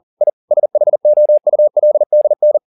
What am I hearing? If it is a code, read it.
Morse code, heard as a keyed tone, about 40 wpm, 600 Hz. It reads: I SHOULDN
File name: the old man and the sea 2 40wpm